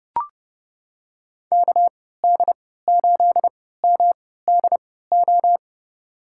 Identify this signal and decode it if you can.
Morse code, heard as a keyed tone, about 32 wpm, spaced as for 17 wpm, 700 Hz. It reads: KD8MDO
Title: USA Call Signs - 6 Characters - 3x Character Spacing - 30wpm